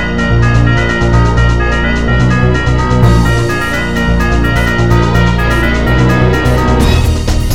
There is no code.